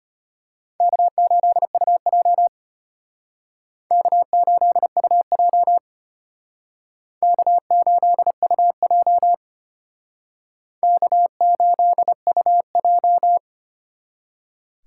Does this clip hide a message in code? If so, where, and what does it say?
Yes: Morse code from 0.8 to 13.4 s, heard as a keyed tone, about 32 wpm, 700 Hz. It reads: K8UJ K8UJ K8UJ K8UJ